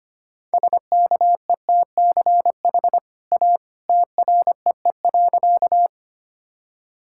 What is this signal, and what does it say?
Morse code, heard as a keyed tone, about 25 wpm, 690 Hz. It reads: SKETCH A TREE.
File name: Easy Sentences with Hints - 25wpm